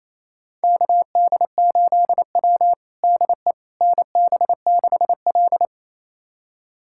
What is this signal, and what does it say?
Morse code, heard as a keyed tone, about 28 wpm, 700 Hz. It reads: KD8W DE NB6L
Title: USA Callsign DE Callsign - 4 Characters - 28wpm